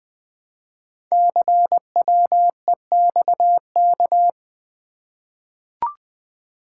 Morse code, heard as a keyed tone, about 20 wpm, 700 Hz. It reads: CWEXK
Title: Code Groups - 5 Letters - 20wpm